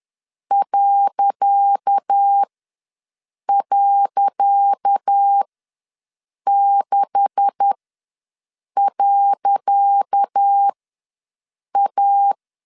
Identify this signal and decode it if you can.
Morse code, heard as a keyed tone, about 10 wpm, spaced as for 6 wpm, 800 Hz. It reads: ..6.A
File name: LEZIONI / 10 SIGNS / 10C DOT ws 10wpm 4213